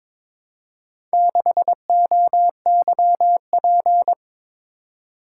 Morse code, heard as a keyed tone, about 22 wpm, 700 Hz. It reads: 6OYP
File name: practice / 22wpm / Four Letter-Number Combinations - 22wpm